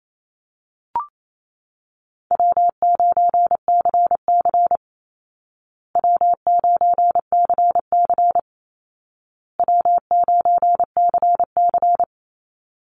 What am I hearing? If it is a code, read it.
Morse code, heard as a keyed tone, about 28 wpm, 700 Hz. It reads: W9CC W9CC W9CC
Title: Easy Call Signs Repeated 3x - 28wpm